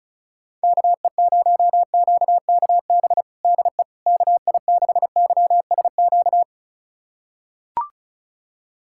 Morse code, heard as a keyed tone, about 34 wpm, 700 Hz. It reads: KE0QKB DE KI6YSQ
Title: USA Callsign DE Callsign - 6 Characters - 35wpm